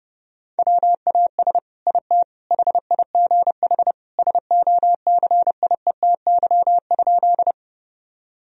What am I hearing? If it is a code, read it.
Morse code, heard as a keyed tone, about 32 wpm, 700 Hz. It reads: WAS IT HIGH SOCIETY?
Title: Sentences from Top 1000 Words - Review Part 4 - 30wpm